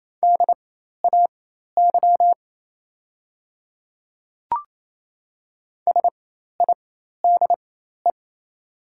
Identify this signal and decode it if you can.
Morse code, heard as a keyed tone, about 28 wpm, spaced as for 13 wpm, 700 Hz. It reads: DAY SIDE
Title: 4x Character Spacing - Top 500 Words - 28wpm